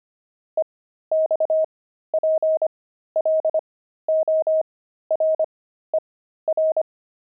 Morse code, heard as a keyed tone, about 25 wpm, spaced as for 13 wpm, 630 Hz. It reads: EXPLORER
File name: pangram-fs-13-cs-25-explorer